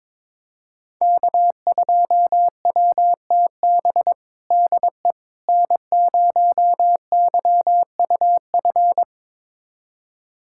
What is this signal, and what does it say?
Morse code, heard as a keyed tone, about 22 wpm, 700 Hz. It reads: K2WTB DE N0YUF
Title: USA Callsign DE Callsign - 5 Characters - 22wpm